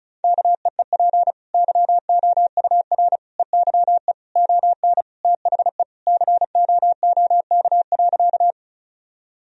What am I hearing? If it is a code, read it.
Morse code, heard as a keyed tone, about 36 wpm, 700 Hz. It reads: KEEP YOUR EYE ON THE COOK.
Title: Sentences from Top 1000 Words - Review Part 4 - 35wpm